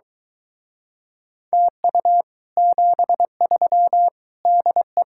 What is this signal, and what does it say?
Morse code, heard as a keyed tone, about 23 wpm, 700 Hz. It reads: TU 73 DE